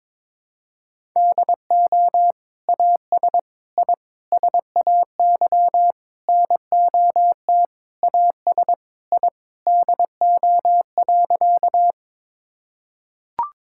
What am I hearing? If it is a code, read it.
Morse code, heard as a keyed tone, about 22 wpm, 700 Hz. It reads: DO AS I SAY NOT AS I DO.
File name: Sentences with 7-Word Phrases - 22wpm